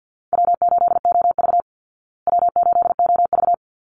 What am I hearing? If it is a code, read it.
Morse code, heard as a keyed tone, about 50 wpm, 700 Hz. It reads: W8OV W8OV
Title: Easy Call Signs Repeated 2x - 50wpm